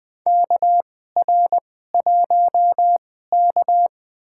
Morse code, heard as a keyed tone, about 20 wpm, spaced as for 14 wpm, 700 Hz. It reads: KR1K
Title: USA Call Signs - 4 Characters - 2x Character Spacing - 20wpm